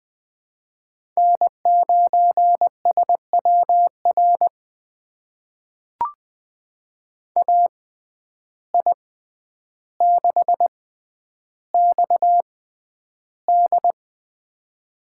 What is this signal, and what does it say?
Morse code, heard as a keyed tone, about 20 wpm, spaced as for 10 wpm, 700 Hz. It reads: N9SWR AI6XD
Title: USA Call Signs - 5 Characters - 6x Character Spacing - 20wpm